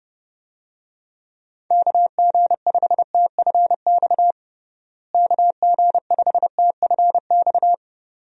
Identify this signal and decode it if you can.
Morse code, heard as a keyed tone, about 30 wpm, 700 Hz. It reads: KG5TFX KG5TFX
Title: USA Call Signs - 6 Characters - Repeated 2x - 30wpm